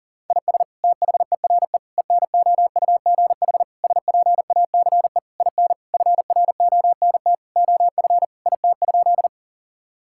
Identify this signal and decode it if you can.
Morse code, heard as a keyed tone, about 40 wpm, 700 Hz. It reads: IS THERE ENOUGH SPACE IN FRONT OF IT?
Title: Sentences from Top 1000 Words - Review Part 5 - 40wpm